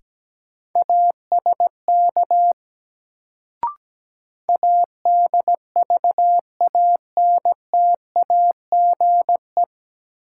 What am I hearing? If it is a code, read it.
Morse code, heard as a keyed tone, about 17 wpm, 700 Hz. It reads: ASK ADVANTAGE
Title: Sets of 4 Words - New Word Drill - 17wpm